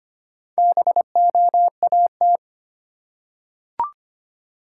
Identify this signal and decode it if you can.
Morse code, heard as a keyed tone, about 25 wpm, 700 Hz. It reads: BOAT
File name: practice / 25wpm / Easy Words with Hints - 25wpm